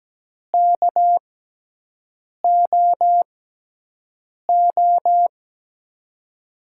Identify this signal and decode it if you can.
Morse code, heard as a keyed tone, about 17 wpm, spaced as for 6 wpm, 700 Hz. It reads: KOO
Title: USA Call Signs - 5 Characters - 6x Character Spacing - 17wpm